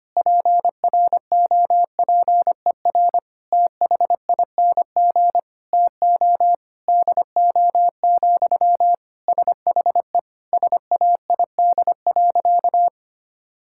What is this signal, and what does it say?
Morse code, heard as a keyed tone, about 25 wpm, 700 Hz. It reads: PROPER THING TO DO, SHE SAID.
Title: Sentences from Top 1000 Words - Review Part 5 - 25wpm